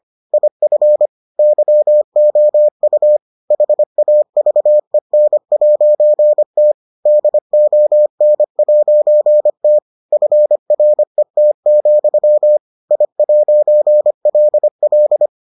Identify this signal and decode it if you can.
Morse code, heard as a keyed tone, about 25 wpm, 600 Hz. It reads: IF YOU HAVEN'T DON'T FRET, I'LL